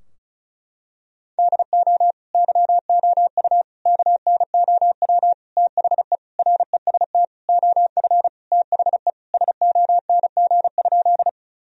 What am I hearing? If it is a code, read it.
Morse code, heard as a keyed tone, about 36 wpm, 700 Hz. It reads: DO YOU KNOW THE REST OF THE SONG?
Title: Sentences from Top 1000 Words - Review Part 2 - 35wpm